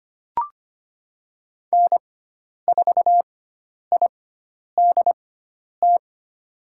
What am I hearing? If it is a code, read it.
Morse code, heard as a keyed tone, about 25 wpm, spaced as for 10 wpm, 700 Hz. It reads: N4IDT
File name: Lesson 16g - Letter c - Call Signs 25wpm